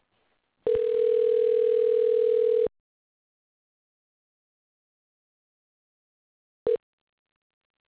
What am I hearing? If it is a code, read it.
Ringback tone (North America), 440 + 480 Hz.
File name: pacbell privacy intercept att ip